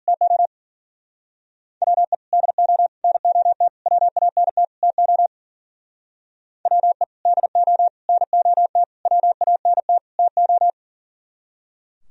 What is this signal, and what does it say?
Morse code, heard as a keyed tone, about 48 wpm, 700 Hz. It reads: TO WE DO NOT WANT TO WE DO NOT WANT TO
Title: Speed-Racing - Sets of 5 Words 40wpm